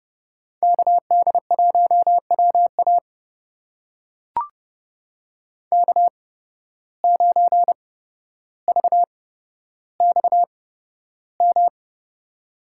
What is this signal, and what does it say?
Morse code, heard as a keyed tone, about 29 wpm, spaced as for 13 wpm, 700 Hz. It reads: KD1WA K9VXM
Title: USA Call Signs - 5 Characters - 8x Character Spacing - 30wpm